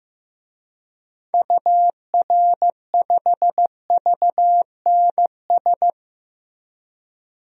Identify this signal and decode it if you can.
Morse code, heard as a keyed tone, about 15 wpm, 700 Hz. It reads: UR5VNS
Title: DXCC-100 Call Signs - 6 Characters - Repeated 2x - 15wpm